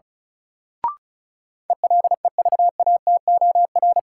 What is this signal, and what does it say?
Morse code, heard as a keyed tone, about 36 wpm, 700 Hz. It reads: ELEVATOR
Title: Sentences from Top 1600 Words - New Word Drill - 35wpm